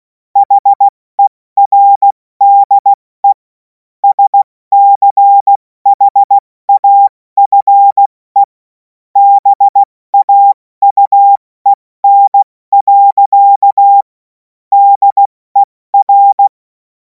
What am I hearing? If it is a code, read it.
Morse code, heard as a keyed tone, about 16 wpm, 800 Hz. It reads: HERDE SCHAFE BAUEN. DER
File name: Morsekurs / Witze / Witz 14. April 2026 70 F80